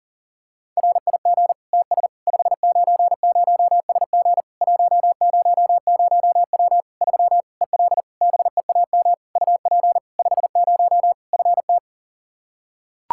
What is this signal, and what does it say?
Morse code, heard as a keyed tone, about 37 wpm, 700 Hz. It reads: RIG TS 590SG 100W 3 EL BEAM UP 50 FT
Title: K1USN SST QSO Practice 40wpm